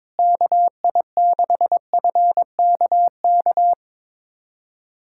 Morse code, heard as a keyed tone, about 21 wpm, 700 Hz. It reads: KI6FKK
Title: USA Callsign DE Callsign - 6 Characters - 22wpm